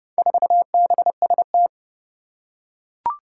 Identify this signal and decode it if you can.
Morse code, heard as a keyed tone, about 30 wpm, 700 Hz. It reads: 4BST
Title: Four Challenging Letter-Number Combinations 30wpm